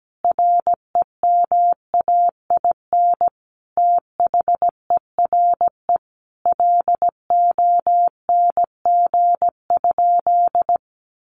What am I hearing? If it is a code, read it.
Morse code, heard as a keyed tone, about 17 wpm, 700 Hz. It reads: REMAIN THERE LONG?
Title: Sentences from Top 1200 Words 17wpm